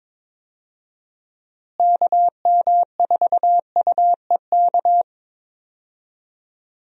Morse code, heard as a keyed tone, about 22 wpm, 700 Hz. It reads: KM4UEK
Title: USA Call Signs - 6 Characters - Repeated 3x - 22wpm